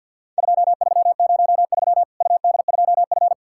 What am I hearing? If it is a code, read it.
Morse code, heard as a keyed tone, about 50 wpm, 700 Hz. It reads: 2303 UD2F
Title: Worked All Europe DX Contest - QTC Exchange - 50wpm